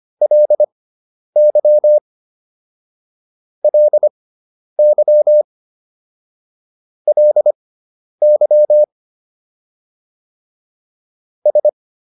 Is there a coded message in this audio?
Yes: Morse code from 0.2 to 11.7 s, heard as a keyed tone, about 25 wpm, spaced as for 10 wpm, 600 Hz. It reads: LY LY LY S